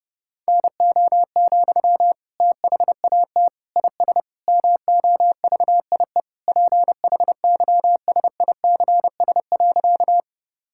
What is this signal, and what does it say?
Morse code, heard as a keyed tone, about 32 wpm, 700 Hz. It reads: NO, THAT IS MOVIE PHYSICS.
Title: Sentences from Top 1000 Words - Review Part 5 - 30wpm